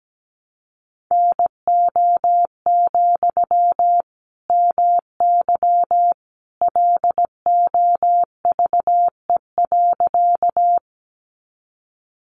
Morse code, heard as a keyed tone, about 17 wpm, 700 Hz. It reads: NO, MY LOVE.
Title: Sentences from Top 400 Words 17wpm